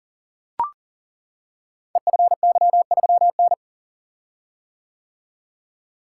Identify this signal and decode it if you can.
Morse code, heard as a keyed tone, about 39 wpm, 690 Hz. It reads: EFY3N
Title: Code Groups - 5 Letter:Numbers - 40wpm